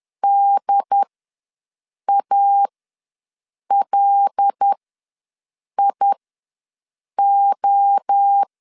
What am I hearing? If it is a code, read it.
Morse code, heard as a keyed tone, about 10 wpm, spaced as for 6 wpm, 800 Hz. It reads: DALIO
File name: LEZIONI / 5 LBVC / 5 LB ws 10wpm 3616